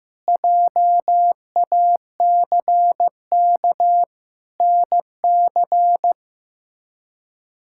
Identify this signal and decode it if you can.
Morse code, heard as a keyed tone, about 16 wpm, 700 Hz. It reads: JACK NC
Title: K1USN SST QSO Practice - 15wpm